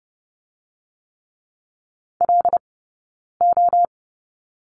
Morse code, heard as a keyed tone, about 30 wpm, 700 Hz. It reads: L O
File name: 092 Lesson 16d - Letter c - Two Characters 30wpm